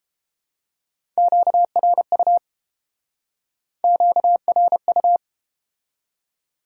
Morse code, heard as a keyed tone, about 30 wpm, 700 Hz. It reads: QRU QRU